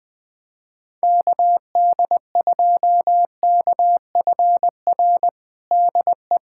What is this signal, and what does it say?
Morse code, heard as a keyed tone, about 20 wpm, 700 Hz. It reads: KD2KFR DE
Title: USA Callsign DE Callsign 20wpm